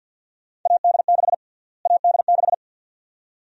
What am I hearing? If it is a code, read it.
Morse code, heard as a keyed tone, about 46 wpm, 700 Hz. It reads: AD6 AD6
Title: USA Call Sign Prefixes - Repeated 2x - 50wpm